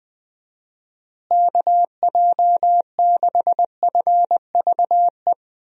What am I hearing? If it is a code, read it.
Morse code, heard as a keyed tone, about 20 wpm, 700 Hz. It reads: KJ6FVE
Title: USA Call Signs - 6 Characters - Repeated 3x - 20wpm